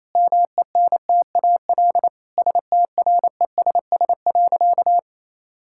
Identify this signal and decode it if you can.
Morse code, heard as a keyed tone, about 27 wpm, 700 Hz. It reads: MENTAL STRESS.
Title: Sentences from Top 1000 Words - Review Part 6 - 28wpm